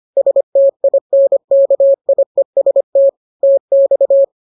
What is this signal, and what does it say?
Morse code, heard as a keyed tone, about 25 wpm, 550 Hz. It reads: STINKIEST TX